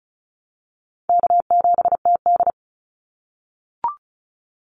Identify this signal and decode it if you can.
Morse code, heard as a keyed tone, about 34 wpm, 700 Hz. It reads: K7TD
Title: Call Signs - 35wpm